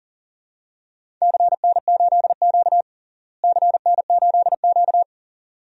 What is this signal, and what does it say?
Morse code, heard as a keyed tone, about 37 wpm, 700 Hz. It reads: CN8Q CN8Q
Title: DXCC-100 Call Signs - 4 Characters - Repeated 2x - 40wpm